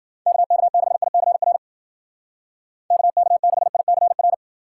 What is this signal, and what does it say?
Morse code, heard as a keyed tone, about 50 wpm, 700 Hz. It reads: KK6ICR KK6ICR